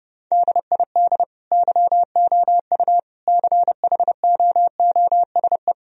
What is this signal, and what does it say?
Morse code, heard as a keyed tone, about 31 wpm, 700 Hz. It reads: DID YOU CHOOSE
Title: Sentences from Top 1000 Words - Review Part 4 - 30wpm